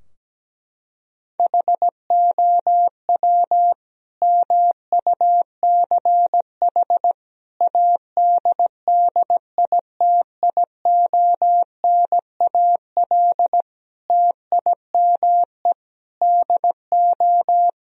Morse code, heard as a keyed tone, about 17 wpm, 700 Hz. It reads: HOW MUCH ADDITIONAL TIME DO
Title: Sentences from Top 1000 Words - Review Part 4 - 17wpm